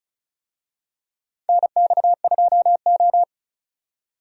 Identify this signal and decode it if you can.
Morse code, heard as a keyed tone, about 36 wpm, 700 Hz. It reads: NX2O
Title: USA Call Signs - 4 Characters - 35wpm